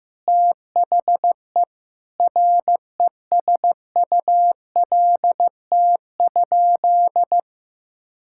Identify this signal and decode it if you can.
Morse code, heard as a keyed tone, about 15 wpm, 700 Hz. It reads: THE RESULT?
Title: Sentences from Top 1600 Words 15wpm